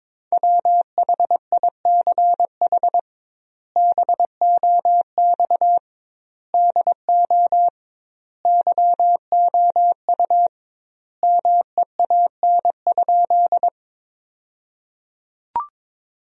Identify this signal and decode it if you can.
Morse code, heard as a keyed tone, about 22 wpm, 700 Hz. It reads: WHICH BOX DO YOU MEAN?
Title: 2x Word-Spacing - Sentences from Top 500 Words - 22wpm